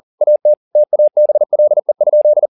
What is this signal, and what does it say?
Morse code, heard as a keyed tone, about 40 wpm, 600 Hz. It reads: AT TABLE?